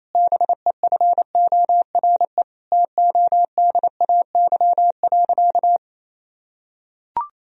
Morse code, heard as a keyed tone, about 28 wpm, 700 Hz. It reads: BEFORE TODAY.
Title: Sentences from Top 1000 Words - Review Part 4 - 28wpm